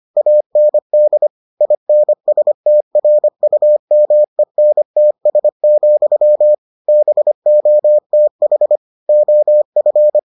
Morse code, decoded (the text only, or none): AND INSTRUMENTS, BOTH OF